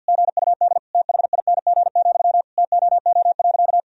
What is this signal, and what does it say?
Morse code, heard as a keyed tone, about 45 wpm, 700 Hz. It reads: GUD THING, TOO.
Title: Sentences from Top 200 Words and Abbreviations 50wpm